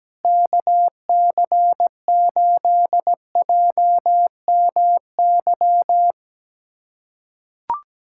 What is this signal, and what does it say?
Morse code, heard as a keyed tone, about 17 wpm, 700 Hz. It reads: KC8JMY
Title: USA Callsign DE Callsign - 6 Characters - 17wpm